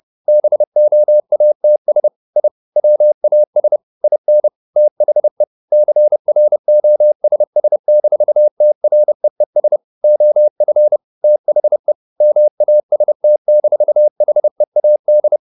Morse code, heard as a keyed tone, about 30 wpm, 600 Hz. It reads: BOATS I WAS IN THE CROSS-TREES OF THE MAST-HEAD